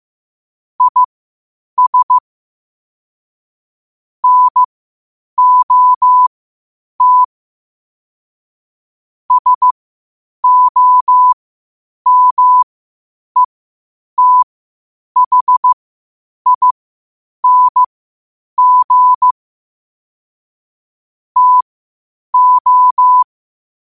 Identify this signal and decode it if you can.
Morse code, heard as a keyed tone, about 15 wpm, spaced as for 8 wpm, 1000 Hz. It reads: IS NOT SOMETHING TO
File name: qotdfiles / 8wpm